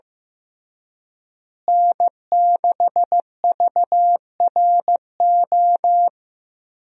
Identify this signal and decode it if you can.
Morse code, heard as a keyed tone, about 15 wpm, 700 Hz. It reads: N6VRO
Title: USA Call Signs - 5 Characters - Repeated 3x - 15wpm